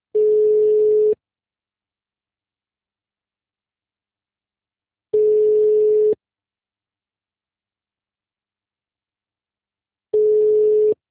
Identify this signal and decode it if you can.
Ringback tone (Europe, 425 Hz).